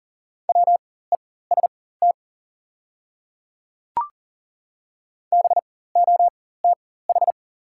Morse code, heard as a keyed tone, about 40 wpm, spaced as for 19 wpm, 690 Hz. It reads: WEST BOTH
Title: 4x Character Spacing - Top 500 Words - 40wpm